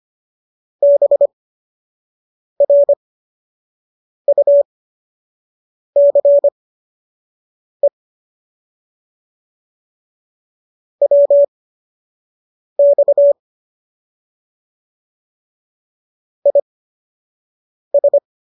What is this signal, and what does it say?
Morse code, heard as a keyed tone, about 25 wpm, spaced as for 6 wpm, 570 Hz. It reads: BRUCE WX IS